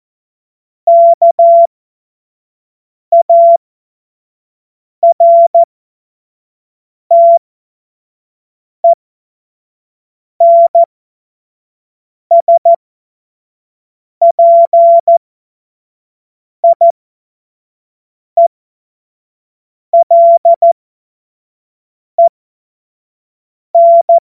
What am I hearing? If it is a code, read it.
Morse code, heard as a keyed tone, about 14 wpm, spaced as for 5 wpm, 680 Hz. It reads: KARTENSPIELEN
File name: Morsekurs / Witze / Witz 04. Oktober 2022 25 F70